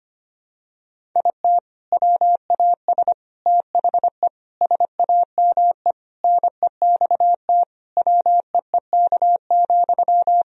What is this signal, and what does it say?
Morse code, heard as a keyed tone, about 25 wpm, 700 Hz. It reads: IT WAS THE SAME NEXT WEEK,
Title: Sentences from Top 500 Words and Abbreviations 25wpm